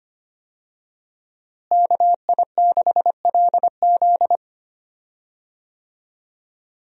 Morse code, heard as a keyed tone, about 25 wpm, 700 Hz. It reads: KI6LZ
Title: USA Call Signs - 5 Characters - Repeated 3x - 25wpm